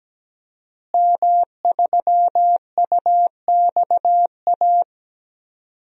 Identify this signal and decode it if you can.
Morse code, heard as a keyed tone, about 17 wpm, 700 Hz. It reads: M3UXA
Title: DXCC-100 Call Signs - 5 Characters - Repeated 2x - 17wpm